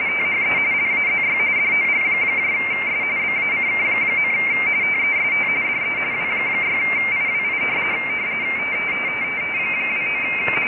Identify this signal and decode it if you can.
RTTY, about 45.45 baud. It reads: THE LAZY DOGS BACK. TEST DE W6IWI 29
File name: rtty / websdr recording 14086.0kHz